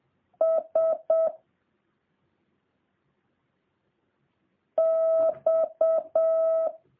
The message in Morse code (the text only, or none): S X